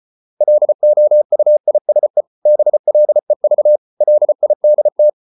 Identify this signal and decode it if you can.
Morse code, heard as a keyed tone, about 34 wpm, 600 Hz. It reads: LOUISE BLEV LIDT
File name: Ferie QRQ 34wpm